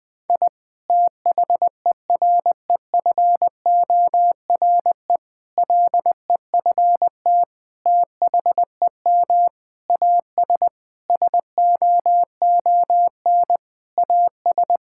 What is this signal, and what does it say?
Morse code, heard as a keyed tone, about 20 wpm, 700 Hz. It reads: I THEREFORE LEFT THEM AS SOON AS
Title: Sentences from Top 1000 Words - Review Part 5 - 20wpm